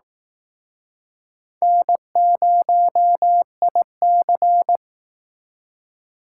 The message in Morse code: N0IC

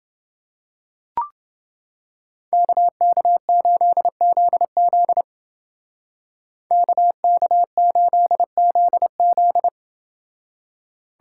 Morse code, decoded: KK8ZZ KK8ZZ

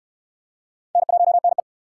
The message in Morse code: N1NE

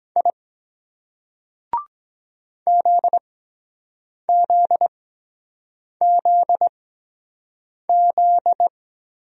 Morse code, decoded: I ZZZZ

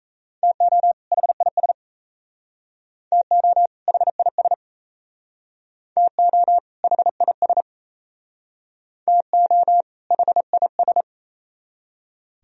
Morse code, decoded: TO HIS TO HIS TO HIS TO HIS

